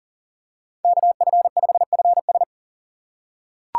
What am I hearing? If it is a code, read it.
Morse code, heard as a keyed tone, about 40 wpm, 700 Hz. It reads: KF5FS
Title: USA Call Signs - 5 Characters - 6x Character Spacing - 40wpm